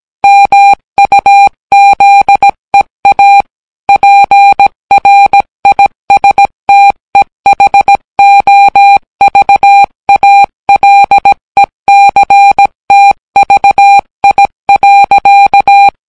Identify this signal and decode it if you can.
Morse code, heard as a keyed tone, about 17 wpm, 800 Hz. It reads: MUZEA PRISTEHOVALECTVI.